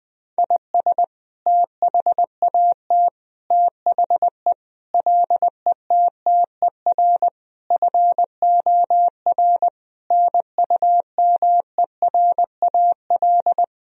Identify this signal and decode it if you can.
Morse code, heard as a keyed tone, about 20 wpm, 700 Hz. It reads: IS THAT THE LETTER FOR NUMERAL
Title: Sentences from Top 1000 Words - Review Part 2 - 20wpm